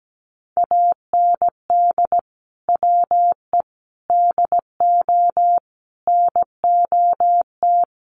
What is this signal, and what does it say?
Morse code, heard as a keyed tone, about 17 wpm, 700 Hz. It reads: AND WE DO NOT
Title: Sets of 5 Words Encore 17wpm